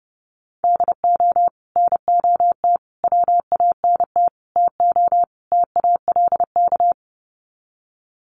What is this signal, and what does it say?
Morse code, heard as a keyed tone, about 30 wpm, 700 Hz. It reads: DO NOT WANT TO TALK
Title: Sets of 5 Words 30wpm